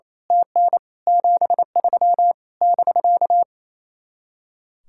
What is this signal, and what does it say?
Morse code, heard as a keyed tone, about 26 wpm, 700 Hz. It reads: TN 73 <BK>
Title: Speed-Racing Common QSO Phrases - 28wpm